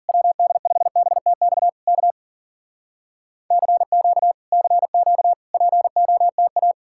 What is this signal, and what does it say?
Morse code, heard as a keyed tone, about 53 wpm, 700 Hz. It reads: WD5BTX K CQ CQ POTA